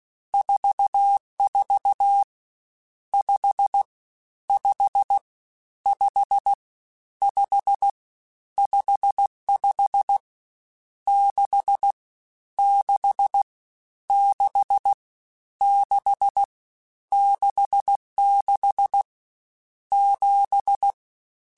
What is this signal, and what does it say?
Morse code, heard as a keyed tone, about 16 wpm, 800 Hz. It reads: 44 5 5 5 5 55 6 6 6 6 66 7
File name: Misc / Misc-mp3s / Numb @15